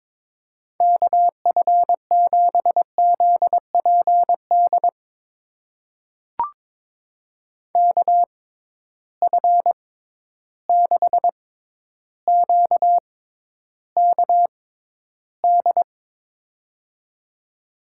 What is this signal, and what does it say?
Morse code, heard as a keyed tone, about 21 wpm, spaced as for 11 wpm, 700 Hz. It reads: KF7ZPD KF6QKD
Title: USA Call Signs - 6 Characters - 6x Character Spacing - 22wpm